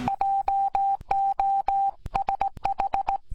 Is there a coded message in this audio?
Yes: Morse code from 0.1 to 3.2 s, heard as a keyed tone, about 19 wpm, 790 Hz. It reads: JOSH